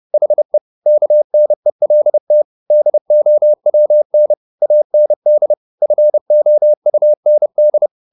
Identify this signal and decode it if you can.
Morse code, heard as a keyed tone, about 30 wpm, 600 Hz. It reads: HE KNELT DOWN AND FOUND
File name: the old man and the sea / the old man and the sea 1 30wpm